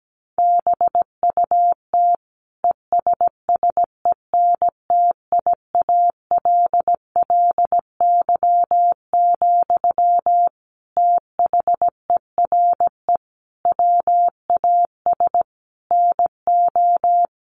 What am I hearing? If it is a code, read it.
Morse code, heard as a keyed tone, about 17 wpm, 700 Hz. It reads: BUT ESSENTIALLY, THERE WAS NO
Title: Sentences from Top 1300 Words 17wpm